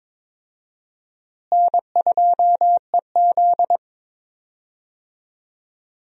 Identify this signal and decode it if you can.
Morse code, heard as a keyed tone, about 22 wpm, 700 Hz. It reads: N2EZ
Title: USA Call Signs - 4 Characters - Repeated 3x - 22wpm